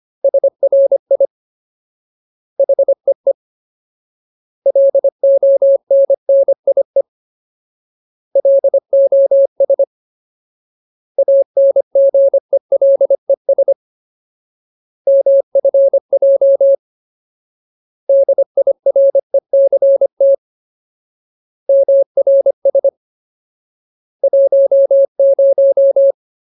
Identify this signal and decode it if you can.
Morse code, heard as a keyed tone, about 25 wpm, 550 Hz. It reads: SRI HEE LONNIE LOS ANGELES MFJ DIRECT MRS 10
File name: Ultimate Word List / 25WPM Ultimate